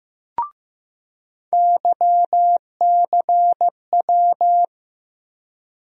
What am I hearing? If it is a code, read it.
Morse code, heard as a keyed tone, about 15 wpm, 700 Hz. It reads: YCW
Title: USA Call Sign Postfixes - 3 Letters 15wpm